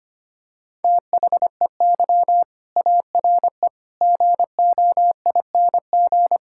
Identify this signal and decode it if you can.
Morse code, heard as a keyed tone, about 25 wpm, 700 Hz. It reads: THEY ARE GOING
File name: practice / 25wpm / Sentences with 4-Word Phrases - 25wpm